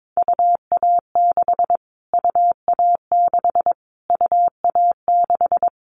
Morse code, heard as a keyed tone, about 22 wpm, 700 Hz. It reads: UA6 UA6 UA6
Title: Callsign Prefixes for Top 25 DXCC - Repeated 3x - 22wpm